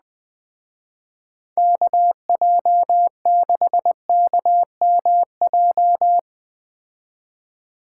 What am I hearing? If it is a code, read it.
Morse code, heard as a keyed tone, about 19 wpm, 700 Hz. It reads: KJ6KMJ
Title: Speed Racing - USA 2x3 Call Signs 17wpm